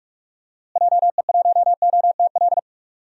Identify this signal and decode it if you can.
Morse code, heard as a keyed tone, about 44 wpm, 700 Hz. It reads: JE1OTL
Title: DXCC-100 Call Signs - 6 Characters - Repeated 2x - 45wpm